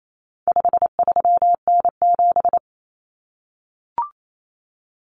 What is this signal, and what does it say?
Morse code, heard as a keyed tone, about 28 wpm, 700 Hz. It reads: 53N7
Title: Common Cut Numbers - 4-Digit 28wpm